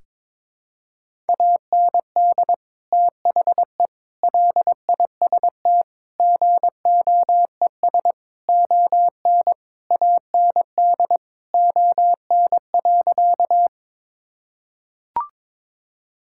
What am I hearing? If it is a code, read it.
Morse code, heard as a keyed tone, about 22 wpm, 700 Hz. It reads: AND THE LIST GOES ON AND ON.
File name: Sentences with 7-Word Phrases - 22wpm